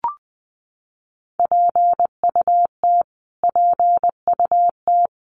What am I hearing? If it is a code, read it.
Morse code, heard as a keyed tone, about 20 wpm, 700 Hz. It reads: PUT PUT
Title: Top 200 Words Repeated 20wpm